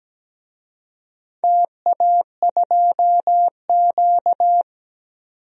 Morse code, heard as a keyed tone, about 17 wpm, 700 Hz. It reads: TA2Q
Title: DXCC-100 Call Signs - 4 Characters - 17wpm